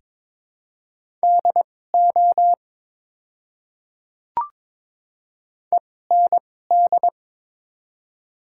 Morse code, heard as a keyed tone, about 21 wpm, spaced as for 16 wpm, 700 Hz. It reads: DO END
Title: 2x Character Spacing - Top 200 Words - 22wpm